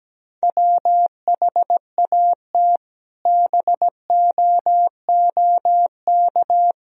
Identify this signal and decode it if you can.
Morse code, heard as a keyed tone, about 17 wpm, 700 Hz. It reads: WHAT BOOK
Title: Sentences from Top 1000 Words - Review Part 1 - 17wpm